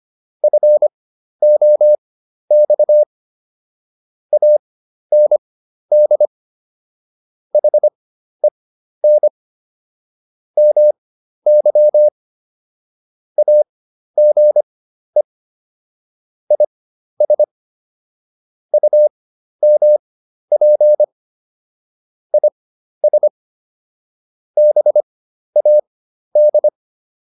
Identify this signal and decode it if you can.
Morse code, heard as a keyed tone, about 25 wpm, spaced as for 12 wpm, 600 Hz. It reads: FOX AND HEN MY AGE IS UMP IS BAD